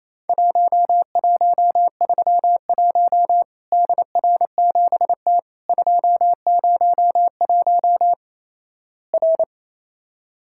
Morse code, heard as a keyed tone, about 28 wpm, 700 Hz. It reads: 1131 DR7T 201 R
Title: Worked All Europe DX Contest - QTC Exchange - 28wpm